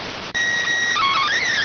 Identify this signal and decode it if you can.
SSTV (slow-scan television), PD 290.